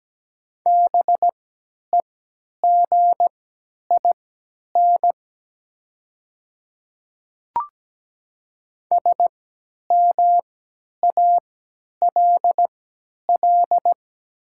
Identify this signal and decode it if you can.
Morse code, heard as a keyed tone, about 17 wpm, spaced as for 10 wpm, 690 Hz. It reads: BEGIN SMALL
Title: 3x Character Spacing - Top 500 Words - 17wpm